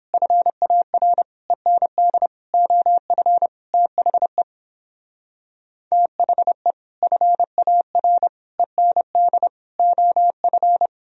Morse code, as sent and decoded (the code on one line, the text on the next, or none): ..-. .- .-. / . -. -.. / --- ..-. / - .... / - .... / ..-. .- .-. / . -. -.. / --- ..-.
FAR END OF THE THE FAR END OF